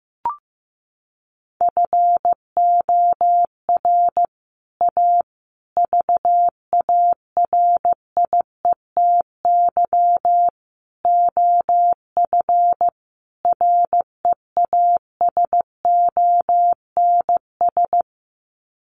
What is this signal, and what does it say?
Morse code, heard as a keyed tone, about 14 wpm, 700 Hz. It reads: FOR A VARIETY OF REASONS
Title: Sets of 5 Words Encore 15wpm